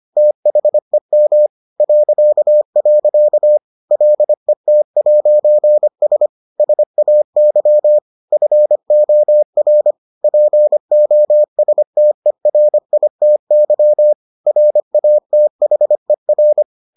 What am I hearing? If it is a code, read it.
Morse code, heard as a keyed tone, about 25 wpm, 600 Hz. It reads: THEM .. LET'S SAY FOR POSTERITY RATHER